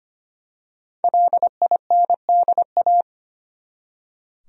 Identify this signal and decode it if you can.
Morse code, heard as a keyed tone, about 25 wpm, 700 Hz. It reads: LINDA
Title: USA Common Names - 25wpm